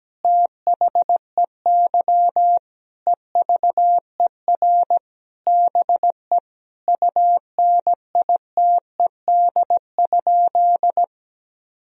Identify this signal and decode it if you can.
Morse code, heard as a keyed tone, about 17 wpm, 700 Hz. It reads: THEY EVER BE UNITED?
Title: Sentences from Top 1000 Words - Review Part 4 - 17wpm